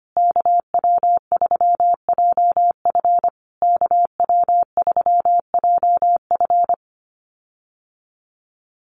Morse code, heard as a keyed tone, about 25 wpm, 700 Hz. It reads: KW3JF KW3JF
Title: USA Call Signs Repeated 3x 25wpm